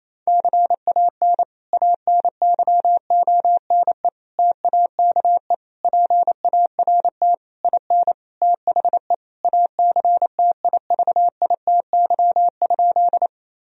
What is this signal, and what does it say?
Morse code, heard as a keyed tone, about 28 wpm, 700 Hz. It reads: CAN ANYONE TAKE PART IN THE ACTIVITY?
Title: Sentences from Top 1000 Words - Review Part 4 - 28wpm